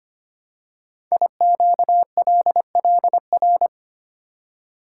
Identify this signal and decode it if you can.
Morse code, heard as a keyed tone, about 25 wpm, 700 Hz. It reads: IQLLR
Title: Code Groups - 5 Letters - 25wpm